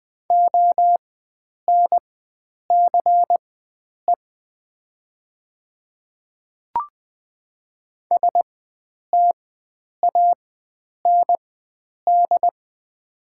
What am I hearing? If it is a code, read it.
Morse code, heard as a keyed tone, about 20 wpm, spaced as for 9 wpm, 700 Hz. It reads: ONCE STAND
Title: 4x Character Spacing - Top 500 Words - 20wpm